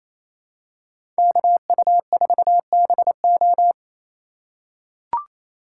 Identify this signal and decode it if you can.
Morse code, heard as a keyed tone, about 28 wpm, 700 Hz. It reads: KU4BO
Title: USA Call Signs - 5 Characters - 4x Character Spacing - 28wpm